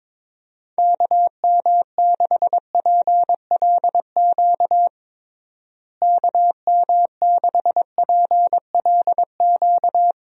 Morse code, heard as a keyed tone, about 22 wpm, 700 Hz. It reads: KM6PLQ KM6PLQ
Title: USA Call Signs - 6 Characters - Repeated 3x - 22wpm